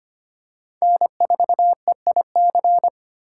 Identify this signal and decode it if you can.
Morse code, heard as a keyed tone, about 25 wpm, 700 Hz. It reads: N4EIC
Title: USA Call Signs - 5 Characters - 4x Character Spacing - 25wpm